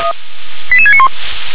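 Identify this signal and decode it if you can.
Five-tone selective call (ZVEI), heard as quick runs of five tones, tones of about 70 ms. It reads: 88691
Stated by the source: ZVEI five-tone call received on SDR.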